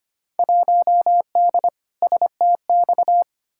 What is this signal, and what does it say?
Morse code, heard as a keyed tone, about 25 wpm, 700 Hz. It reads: 1D STX
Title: Field Day Exchanges - 25wpm